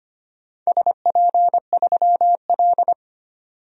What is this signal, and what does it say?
Morse code, heard as a keyed tone, about 25 wpm, 700 Hz. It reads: SP3L
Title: DXCC-100 Call Signs - 4 Characters - 25wpm